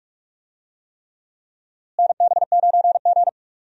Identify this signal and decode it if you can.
Morse code, heard as a keyed tone, about 43 wpm, 700 Hz. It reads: NB9G
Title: USA Call Signs - 4 Characters - Repeated 3x - 45wpm